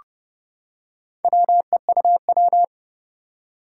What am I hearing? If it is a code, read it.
Morse code, heard as a keyed tone, about 30 wpm, 700 Hz. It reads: WEUW
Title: Four Challenging Letter-Number Combinations 30wpm